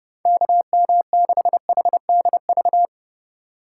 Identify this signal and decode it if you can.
Morse code, heard as a keyed tone, about 31 wpm, 700 Hz. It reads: KM6HDV